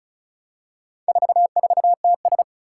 Morse code, heard as a keyed tone, about 36 wpm, 700 Hz. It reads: 44TS
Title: Four Challenging Letter-Number Combinations 35wpm